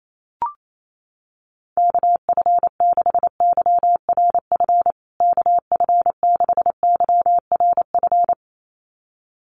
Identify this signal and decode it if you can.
Morse code, heard as a keyed tone, about 28 wpm, 700 Hz. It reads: KF6YRF KF6YRF
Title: USA Call Signs Repeated 2x 28wpm